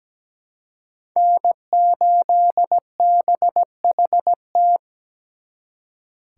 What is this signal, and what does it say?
Morse code, heard as a keyed tone, about 20 wpm, 700 Hz. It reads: N8BHT
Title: Speed Racing - USA Call Signs 17wpm